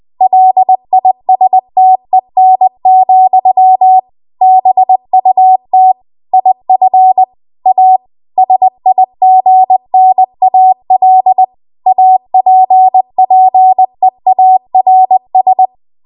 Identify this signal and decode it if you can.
Morse code, heard as a keyed tone, about 20 wpm, 750 Hz. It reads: LISTEN, BUT IF A SIGNAL APPEARS
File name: Archive / 20 WPM / 20WPM